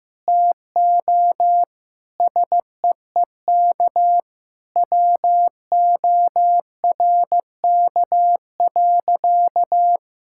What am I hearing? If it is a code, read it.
Morse code, heard as a keyed tone, about 15 wpm, 700 Hz. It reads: TO SEEK WORK.
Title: Sentences from Top 1600 Words 15wpm